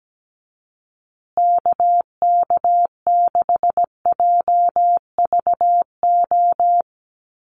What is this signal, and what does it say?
Morse code, heard as a keyed tone, about 17 wpm, 700 Hz. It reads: KK6JVO
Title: USA Call Signs Repeated 3x 17wpm